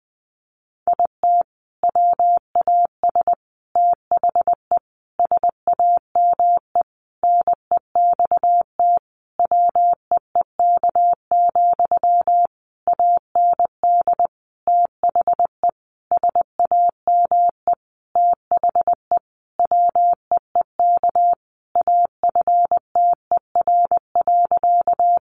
Morse code, heard as a keyed tone, about 20 wpm, 700 Hz. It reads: IT WAS THE SAME NEXT WEEK, AND THE SAME THE WEEK AFTER.
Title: Sentences from Top 500 Words 20wpm